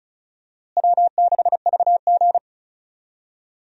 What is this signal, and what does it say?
Morse code, heard as a keyed tone, about 36 wpm, 700 Hz. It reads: W6VG
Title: USA Call Signs - 4 Characters - Repeated 2x - 35wpm